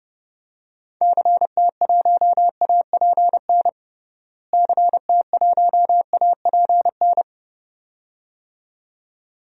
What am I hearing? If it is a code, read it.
Morse code, heard as a keyed tone, about 32 wpm, 700 Hz. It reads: CT1APN CT1APN
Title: DXCC-100 Call Signs - 6 Characters - Repeated 2x - 30wpm